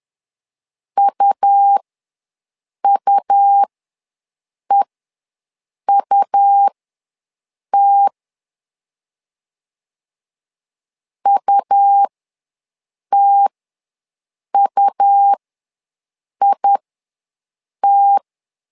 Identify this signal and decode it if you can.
Morse code, heard as a keyed tone, about 11 wpm, spaced as for 6 wpm, 800 Hz. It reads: UUEUT UTUIT
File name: LEZIONI / 2 UHDN / 2 U ws 10wpm 2017